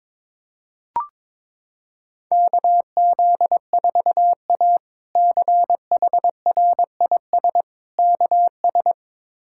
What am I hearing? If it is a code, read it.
Morse code, heard as a keyed tone, about 22 wpm, 700 Hz. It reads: KZ4A CHRIS KS
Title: K1USN SST QSO Practice - 22wpm